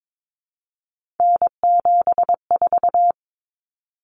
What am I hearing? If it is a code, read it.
Morse code, heard as a keyed tone, about 22 wpm, 700 Hz. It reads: N74
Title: Common Cut Numbers - 3-Digit 22wpm